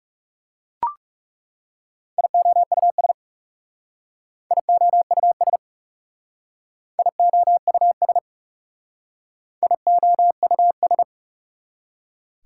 Morse code, heard as a keyed tone, about 39 wpm, 700 Hz. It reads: IOUS IOUS IOUS IOUS